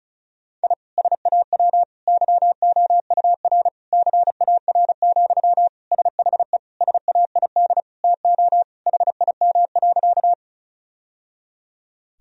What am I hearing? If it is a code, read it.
Morse code, heard as a keyed tone, about 36 wpm, 700 Hz. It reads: I SAW YOUR CAR, SHE SAID TO HIM.
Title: Sentences from Top 1000 Words - Review Part 1 - 35wpm